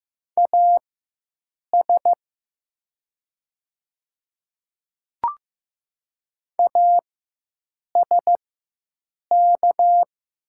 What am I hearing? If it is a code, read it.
Morse code, heard as a keyed tone, about 15 wpm, spaced as for 7 wpm, 700 Hz. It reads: AS ASK
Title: 4x Character Spacing - Top 200 Words - 15wpm